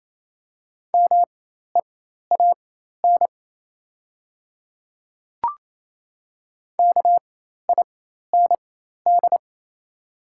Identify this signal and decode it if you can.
Morse code, heard as a keyed tone, about 28 wpm, spaced as for 13 wpm, 700 Hz. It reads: MEAN KIND
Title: 4x Character Spacing - Top 300 Words - 28wpm